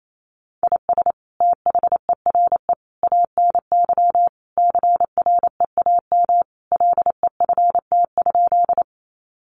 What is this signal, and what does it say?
Morse code, heard as a keyed tone, about 28 wpm, 700 Hz. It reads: IS THERE ANY CREAM LEFT?
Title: Sentences from Top 1500 Words 28wpm